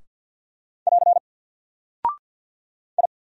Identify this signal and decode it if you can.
Morse code, heard as a keyed tone, about 48 wpm, 700 Hz. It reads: <AR> I
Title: Single Character (Letter:Number:Punctuation:Prosign) Alt 50wpm